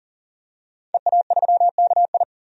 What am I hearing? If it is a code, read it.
Morse code, heard as a keyed tone, about 40 wpm, 700 Hz. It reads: EA3KI
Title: DXCC-100 Call Signs - 5 Characters - 40wpm